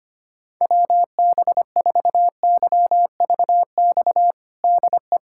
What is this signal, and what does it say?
Morse code, heard as a keyed tone, about 25 wpm, 700 Hz. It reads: WB4YVX DE
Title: USA Callsign DE Callsign - 6 Characters - 25wpm